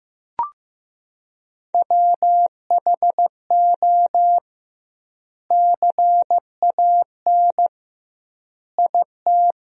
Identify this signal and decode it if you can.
Morse code, heard as a keyed tone, about 15 wpm, 700 Hz. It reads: WHO CAN IT
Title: 2x Word-Spacing - Sentences from Top 100 Words - 15wpm